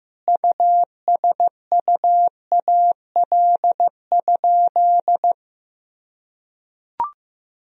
Morse code, heard as a keyed tone, about 15 wpm, 700 Hz. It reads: USUAL?
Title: Sentences from Top 1000 Words - Review Part 2 - 15wpm